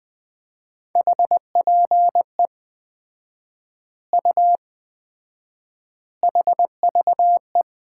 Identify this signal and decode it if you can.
Morse code, heard as a keyed tone, about 20 wpm, 700 Hz. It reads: HPE U HVE
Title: Common QSO Phrases - 4x Word-Spacing - 20wpm